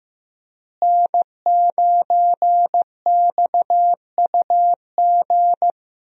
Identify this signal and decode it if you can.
Morse code, heard as a keyed tone, about 15 wpm, 700 Hz. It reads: N9XUG
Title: USA Call Signs - 5 Characters - Repeated 3x - 15wpm